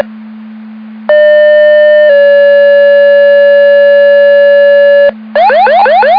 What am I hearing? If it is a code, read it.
Two-tone page, heard as a one-second tone, then a three-second tone, about 600 then 570 Hz. It is heard from 1.1 to 5.1 s.